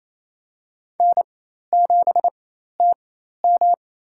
Morse code, heard as a keyed tone, about 28 wpm, spaced as for 13 wpm, 700 Hz. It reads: N7TM
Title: USA Call Signs - 4x Character Spacing - 28wpm